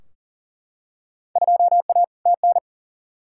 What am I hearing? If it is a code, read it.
Morse code, heard as a keyed tone, about 40 wpm, 700 Hz. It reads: 2A TN
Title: Field Day Exchanges - 40wpm